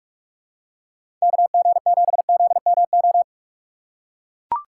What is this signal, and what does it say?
Morse code, heard as a keyed tone, about 44 wpm, 700 Hz. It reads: KG7ZMO
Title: USA Call Signs - 6 Characters - 6x Character Spacing - 45wpm